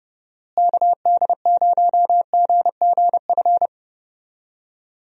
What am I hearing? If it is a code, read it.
Morse code, heard as a keyed tone, about 32 wpm, 700 Hz. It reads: KD0GGF
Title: USA Call Signs - 6 Characters - Repeated 2x - 30wpm